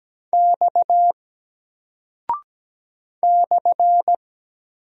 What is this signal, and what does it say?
Morse code, heard as a keyed tone, about 17 wpm, 700 Hz. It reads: X /
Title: Single Character (Letter:Number:Punctuation:Prosign) 17wpm